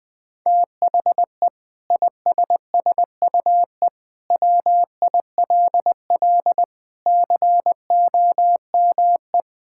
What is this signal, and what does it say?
Morse code, heard as a keyed tone, about 20 wpm, 700 Hz. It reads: THE ISSUE WILL COME
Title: Sentences from Top 1000 Words - Review Part 6 - 20wpm